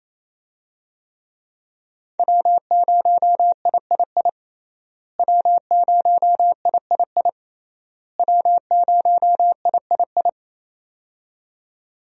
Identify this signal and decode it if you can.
Morse code, heard as a keyed tone, about 27 wpm, 700 Hz. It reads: W0III W0III W0III
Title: USA Call Signs - 5 Characters - Repeated 3x - 28wpm